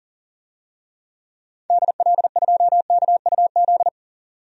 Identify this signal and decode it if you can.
Morse code, heard as a keyed tone, about 40 wpm, 700 Hz. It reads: DL2KUZ
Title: DXCC-100 Call Signs - 6 Characters - 40wpm